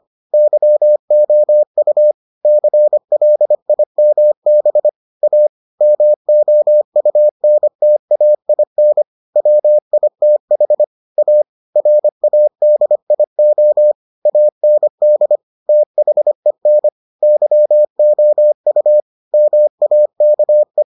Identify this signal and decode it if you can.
Morse code, heard as a keyed tone, about 25 wpm, 600 Hz. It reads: YOU CLIMB A MOUNTAIN WITH A RADIO AND THEN YOU MAKE